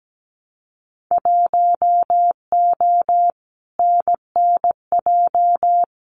Morse code, heard as a keyed tone, about 17 wpm, 700 Hz. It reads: 1O NNJ
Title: Winter Field Day Exchanges 17wpm